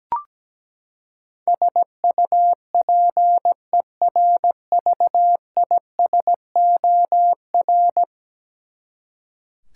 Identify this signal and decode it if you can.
Morse code, heard as a keyed tone, about 17 wpm, 700 Hz. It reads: SUPERVISOR
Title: Practice QSO - Exchange 3 - More Info - 17wpm